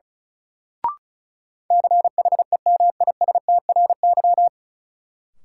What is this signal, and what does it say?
Morse code, heard as a keyed tone, about 36 wpm, 700 Hz. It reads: CHEMISTRY
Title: Sentences from Top 1000 Words - Review Part 6 - New Word Drill - 35wpm